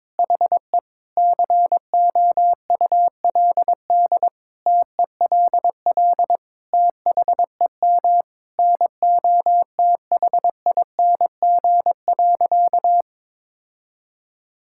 Morse code, heard as a keyed tone, about 22 wpm, 700 Hz. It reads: HE COULD TELL THEM NOTHING.
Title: Sentences from Top 1000 Words - Review Part 6 - 22wpm